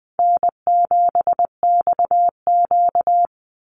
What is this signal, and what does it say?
Morse code, heard as a keyed tone, about 20 wpm, 700 Hz. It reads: N7XQ
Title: USA Call Signs 20wpm